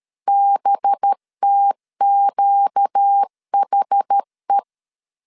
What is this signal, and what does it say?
Morse code, heard as a keyed tone, about 13 wpm, 800 Hz. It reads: BTQHE
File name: CW / DET / 1 6 1500C S A 12wpm 3440